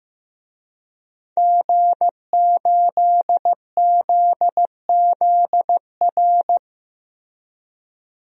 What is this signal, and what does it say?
Morse code, heard as a keyed tone, about 15 wpm, 700 Hz. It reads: G8ZZR
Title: DXCC-100 Call Signs - 5 Characters - Repeated 3x - 15wpm